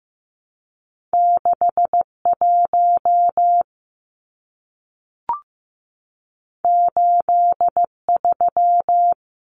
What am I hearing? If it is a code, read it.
Morse code, heard as a keyed tone, about 15 wpm, 700 Hz. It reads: 6 1 8 3
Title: Numbers - 2-Digit 15wpm